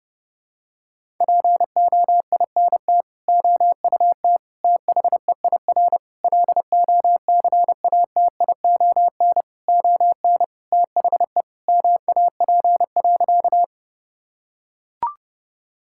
Morse code, heard as a keyed tone, about 30 wpm, 700 Hz. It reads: POINT OUT THEIR LOCATION ON THE MAP.
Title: Sentences from Top 1000 Words - Review Part 6 - 30wpm